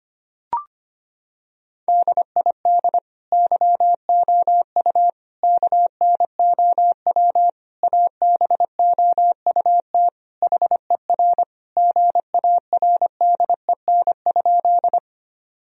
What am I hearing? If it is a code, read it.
Morse code, heard as a keyed tone, about 25 wpm, 700 Hz. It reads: DID YOU KNOW ABOUT HER GARDEN?
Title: Sentences from Top 1000 Words - Review Part 4 - 25wpm